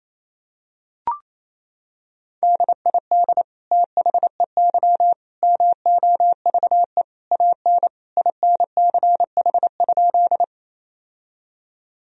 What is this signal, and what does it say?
Morse code, heard as a keyed tone, about 28 wpm, 700 Hz. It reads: DID THEY MOVE AN INCH?
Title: Sentences from Top 1000 Words - Review Part 3 - 28wpm